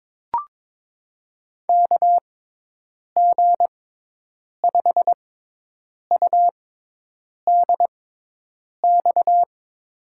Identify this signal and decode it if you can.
Morse code, heard as a keyed tone, about 22 wpm, spaced as for 8 wpm, 700 Hz. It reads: KG5UDX
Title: USA Call Signs - 6 Characters - 6x Character Spacing - 22wpm